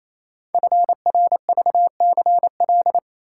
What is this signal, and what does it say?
Morse code, heard as a keyed tone, about 28 wpm, 700 Hz. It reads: FRVCL
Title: Code Groups - 5 Letters - 28wpm